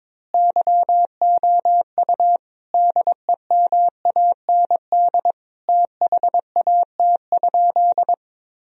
Morse code, heard as a keyed tone, about 22 wpm, 700 Hz. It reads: YOU DEMAND THAT?
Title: Sentences from Top 1000 Words - Review Part 4 - 22wpm